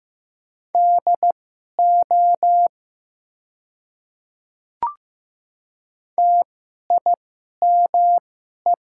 Morse code, heard as a keyed tone, about 15 wpm, spaced as for 11 wpm, 690 Hz. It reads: DO TIME
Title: 2x Character Spacing - Top 100 Words - 15wpm